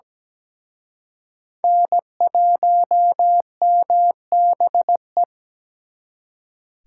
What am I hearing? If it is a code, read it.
Morse code, heard as a keyed tone, about 20 wpm, 700 Hz. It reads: N1MBE
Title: Speed Racing - USA 1x3 Call Signs 17wpm